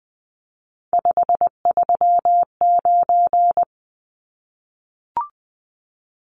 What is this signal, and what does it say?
Morse code, heard as a keyed tone, about 20 wpm, 700 Hz. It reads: 539
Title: Numbers - 3-Digit 20wpm